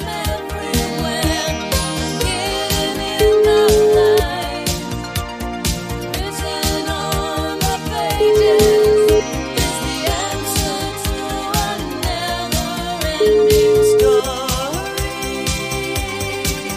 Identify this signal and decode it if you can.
Ringback tone (Europe, 425 Hz), with noise.